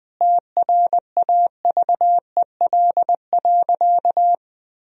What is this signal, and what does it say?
Morse code, heard as a keyed tone, about 20 wpm, 700 Hz. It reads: TRAVEL.
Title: Sentences from Top 1000 Words - Review Part 2 - 20wpm